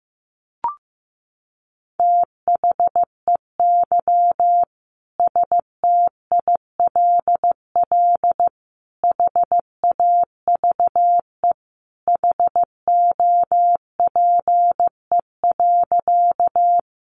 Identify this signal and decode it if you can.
Morse code, heard as a keyed tone, about 15 wpm, 700 Hz. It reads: THEY STILL HAVE HOPE.
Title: Sentences from Top 1000 Words 15wpm